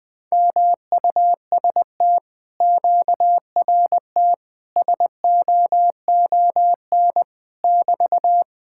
Morse code, heard as a keyed tone, about 20 wpm, 700 Hz. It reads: MUST QRT SOON <BT>
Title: Speed-Racing Common QSO Phrases - 17wpm